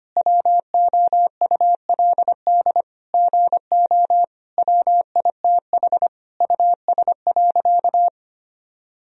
Morse code, heard as a keyed tone, about 25 wpm, 700 Hz. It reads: WOULD GO WITH US.
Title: Sentences from Top 1000 Words - Review Part 3 - 25wpm